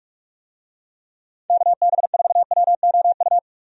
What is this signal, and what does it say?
Morse code, heard as a keyed tone, about 47 wpm, 700 Hz. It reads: KB4WOU
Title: USA Call Signs - 6 Characters - 45wpm